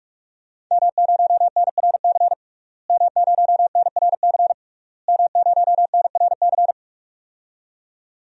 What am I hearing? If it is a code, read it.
Morse code, heard as a keyed tone, about 42 wpm, 700 Hz. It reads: M0NRC M0NRC M0NRC
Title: DXCC-100 Call Signs - 5 Characters - Repeated 3x - 45wpm